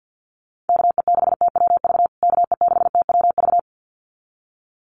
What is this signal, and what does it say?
Morse code, heard as a keyed tone, about 50 wpm, 700 Hz. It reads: KE6TWV KE6TWV
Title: USA Call Signs Repeated 2x 50wpm